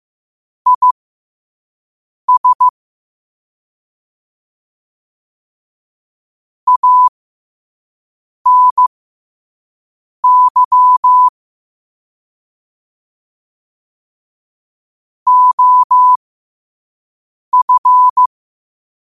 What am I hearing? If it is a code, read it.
Morse code, heard as a keyed tone, about 15 wpm, spaced as for 5 wpm, 1000 Hz. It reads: IS ANY OF